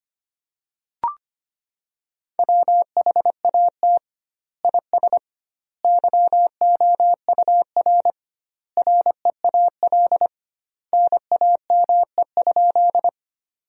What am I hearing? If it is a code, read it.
Morse code, heard as a keyed tone, about 25 wpm, 700 Hz. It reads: WHAT IS YOUR REAL NAME?
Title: 2x Word-Spacing - Sentences from Top 300 Words - 25wpm